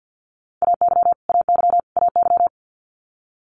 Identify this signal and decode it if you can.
Morse code, heard as a keyed tone, about 50 wpm, 700 Hz. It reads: AY AY AY
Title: Multi-letter Phonemes Repeated 3x 50wpm